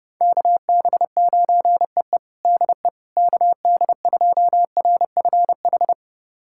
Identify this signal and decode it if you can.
Morse code, heard as a keyed tone, about 29 wpm, 700 Hz. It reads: KB9EE DE KD2RFH
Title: USA Callsign DE Callsign 30wpm